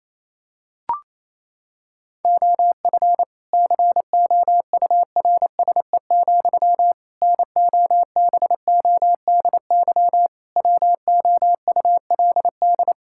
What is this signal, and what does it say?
Morse code, heard as a keyed tone, about 27 wpm, 700 Hz. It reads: OF COURSE, NOBODY WOULD
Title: Sentences from Top 1600 Words 28wpm